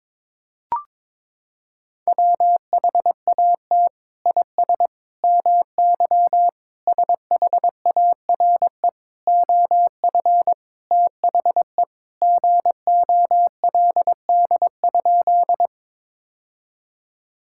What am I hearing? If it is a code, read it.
Morse code, heard as a keyed tone, about 21 wpm, 700 Hz. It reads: WHAT IS MY SHARE OF THE GOLD?
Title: Sentences from Top 1000 Words - Review Part 6 - 22wpm